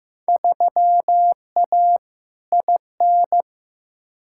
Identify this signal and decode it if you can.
Morse code, heard as a keyed tone, about 15 wpm, 700 Hz. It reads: 3A IN
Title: Field Day Exchanges - 15wpm